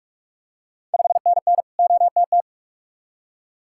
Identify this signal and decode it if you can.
Morse code, heard as a keyed tone, about 46 wpm, 690 Hz. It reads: 5NN OTT
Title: Ontario QSO Party Exchange 45wpm